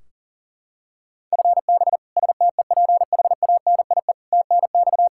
Morse code, heard as a keyed tone, about 40 wpm, 700 Hz. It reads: FB STEPHANIE TNX
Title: Practice QSO - Exchange 3 - More Info - 40wpm